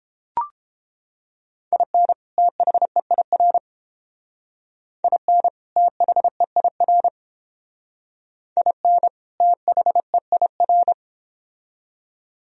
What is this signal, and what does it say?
Morse code, heard as a keyed tone, about 30 wpm, 700 Hz. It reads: IN THEIR IN THEIR IN THEIR